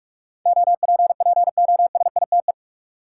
Morse code, heard as a keyed tone, about 41 wpm, 700 Hz. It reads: OPPOSITE
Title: Sentences from Top 1300 Words - New Word Drill - 45wpm